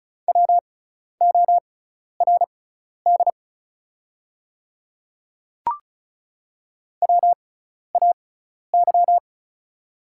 Morse code, heard as a keyed tone, about 36 wpm, spaced as for 12 wpm, 700 Hz. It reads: WORD WAY
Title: 6x Character Spacing - Top 200 Words - 35wpm